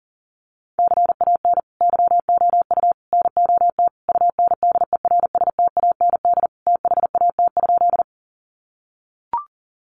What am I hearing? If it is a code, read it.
Morse code, heard as a keyed tone, about 37 wpm, 700 Hz. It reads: CAN YOU NOT UNDERSTAND THAT?
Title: Sentences from Top 600 Words 40wpm